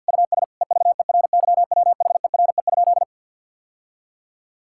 Morse code, heard as a keyed tone, about 46 wpm, 700 Hz. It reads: US EVERYWHERE?
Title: Sentences from Top 1600 Words 50wpm